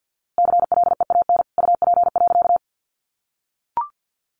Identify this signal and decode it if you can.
Morse code, heard as a keyed tone, about 48 wpm, 700 Hz. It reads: CLEAN UP.
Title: A Sound Of Thunder 50wpm